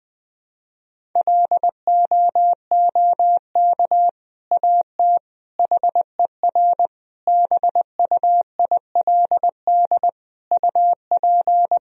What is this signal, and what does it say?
Morse code, heard as a keyed tone, about 20 wpm, 700 Hz. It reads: LOOK AT HER BUILD UP
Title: Sentences from Top 1000 Words - Review Part 1 - 20wpm